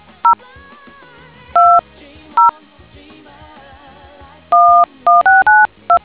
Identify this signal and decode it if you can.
DTMF, heard as telephone touch tones, tones of about 180 ms. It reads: 0 2 0 11692